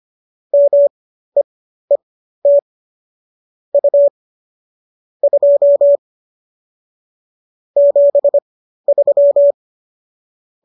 Morse code, heard as a keyed tone, about 25 wpm, spaced as for 13 wpm, 570 Hz. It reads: MEET U 2 73
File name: qso207 13